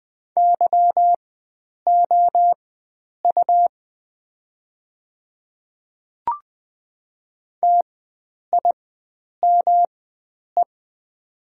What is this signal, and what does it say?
Morse code, heard as a keyed tone, about 20 wpm, spaced as for 9 wpm, 700 Hz. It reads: YOU TIME
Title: 4x Character Spacing - Top 300 Words - 20wpm